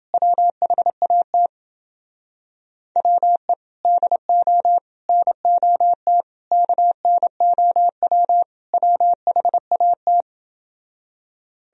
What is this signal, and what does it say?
Morse code, heard as a keyed tone, about 31 wpm, 700 Hz. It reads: WHAT WE DO NOT KNOW WHAT